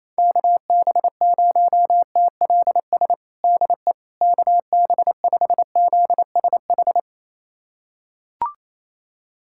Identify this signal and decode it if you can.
Morse code, heard as a keyed tone, about 28 wpm, 700 Hz. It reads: KB0TLS DE KB5ZSH
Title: USA Callsign DE Callsign - 6 Characters - 28wpm